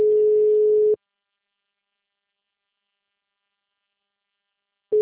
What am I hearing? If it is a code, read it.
Ringback tone (Europe, 425 Hz).
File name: out- -2017- - - .74253